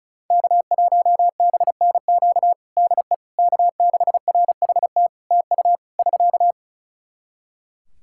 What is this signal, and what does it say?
Morse code, heard as a keyed tone, about 34 wpm, 700 Hz. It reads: K1BNQ DE K6RHT TU <SK>
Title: K1USN SST QSO Practice 35wpm